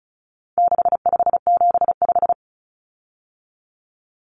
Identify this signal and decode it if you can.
Morse code, heard as a keyed tone, about 34 wpm, 700 Hz. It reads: 6575
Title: Common Cut Numbers - 4-Digit 35wpm